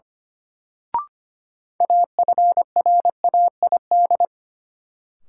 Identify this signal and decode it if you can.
Morse code, heard as a keyed tone, about 25 wpm, 700 Hz. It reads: AFRAID
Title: Sentences from Top 1500 Words - New Word Drill - 25wpm